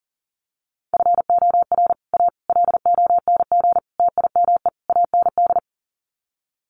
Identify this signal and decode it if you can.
Morse code, heard as a keyed tone, about 40 wpm, 700 Hz. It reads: FOR A LONG TIME AND
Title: Sets of 5 Words Encore2 40wpm